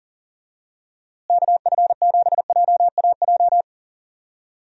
Morse code, heard as a keyed tone, about 40 wpm, 700 Hz. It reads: KF7JAJ
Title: USA Call Signs - 6 Characters - Repeated 2x - 40wpm